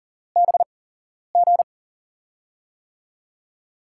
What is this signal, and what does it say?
Morse code, heard as a keyed tone, about 40 wpm, spaced as for 11 wpm, 700 Hz. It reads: BG
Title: USA Call Signs - 6 Characters - 8x Character Spacing - 40wpm